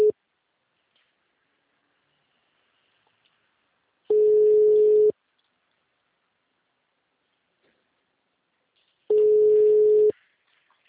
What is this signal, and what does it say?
Ringback tone (Europe, 425 Hz).